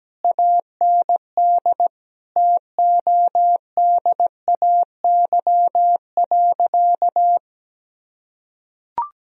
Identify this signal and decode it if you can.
Morse code, heard as a keyed tone, about 17 wpm, 700 Hz. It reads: AND TODAY.
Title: Sentences from Top 1000 Words - Review Part 4 - 17wpm